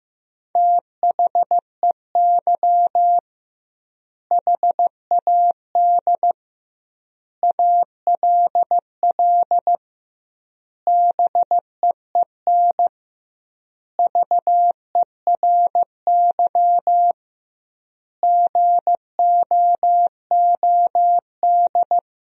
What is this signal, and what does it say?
Morse code, heard as a keyed tone, about 15 wpm, 700 Hz. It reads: THEY HAD ALL BEEN VERY GOOD
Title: 2x Word-Spacing - Sentences from Top 200 Words - 15wpm